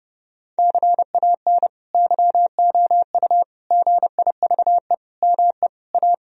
Morse code, heard as a keyed tone, about 32 wpm, 700 Hz. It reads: CAN YOU GIVE ME A
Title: Sentences from Top 1000 Words - Review Part 2 - 30wpm